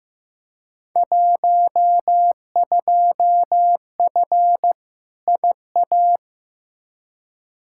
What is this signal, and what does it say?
Morse code, heard as a keyed tone, about 15 wpm, 700 Hz. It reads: 12F IA
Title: Field Day Exchanges Encore - 15wpm